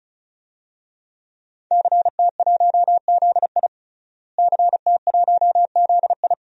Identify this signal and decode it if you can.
Morse code, heard as a keyed tone, about 36 wpm, 700 Hz. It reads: CT1ZI CT1ZI
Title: DXCC-100 Call Signs - 5 Characters - Repeated 2x - 35wpm